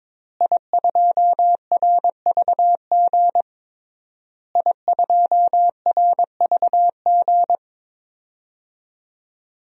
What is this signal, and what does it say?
Morse code, heard as a keyed tone, about 22 wpm, 700 Hz. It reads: I2RVG I2RVG
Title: DXCC-100 Call Signs - 5 Characters - Repeated 3x - 22wpm